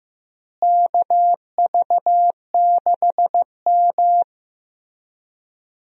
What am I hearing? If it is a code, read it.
Morse code, heard as a keyed tone, about 15 wpm, 700 Hz. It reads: KV6M
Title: USA Call Signs - 4 Characters - Repeated 3x - 15wpm